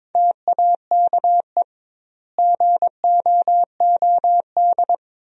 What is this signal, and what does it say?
Morse code, heard as a keyed tone, about 22 wpm, 700 Hz. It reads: TAKE GOOD
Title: 2x Word-Spacing - Sentences from Top 400 Words - 22wpm